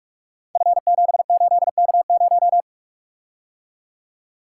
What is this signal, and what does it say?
Morse code, heard as a keyed tone, about 44 wpm, 690 Hz. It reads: F78K0
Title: Code Groups - 5 Letter:Numbers - 45wpm